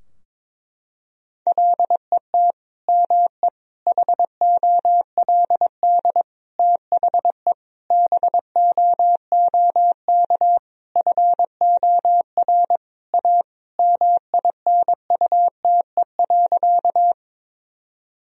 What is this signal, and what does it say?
Morse code, heard as a keyed tone, about 22 wpm, 700 Hz. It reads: LET ME HOLD THE BOOK FOR A MINUTE.
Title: Sentences from Top 1000 Words - Review Part 2 - 22wpm